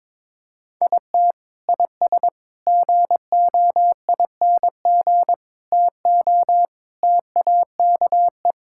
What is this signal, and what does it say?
Morse code, heard as a keyed tone, about 22 wpm, 700 Hz. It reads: IT IS GOING TO TAKE
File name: practice / 22wpm / Sentences with 3-Word Phrases - 22wpm